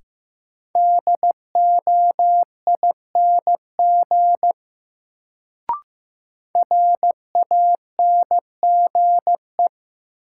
Morse code, heard as a keyed tone, about 15 wpm, 700 Hz. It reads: DOING RANGE
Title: Sets of 4 Words - New Word Drill - 15wpm